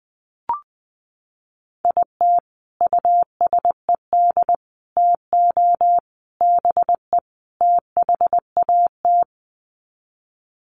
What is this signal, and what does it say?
Morse code, heard as a keyed tone, about 20 wpm, 700 Hz. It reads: IT USED TO BE THAT
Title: Sets of 5 Words Encore2 20wpm